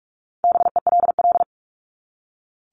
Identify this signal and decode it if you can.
Morse code, heard as a keyed tone, about 44 wpm, 700 Hz. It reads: BELL
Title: Top 1751-2000 Words 45wpm